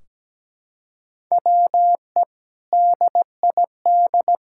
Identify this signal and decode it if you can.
Morse code, heard as a keyed tone, about 17 wpm, 700 Hz. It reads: WE DID
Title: Sentences from Top 1600 Words 17wpm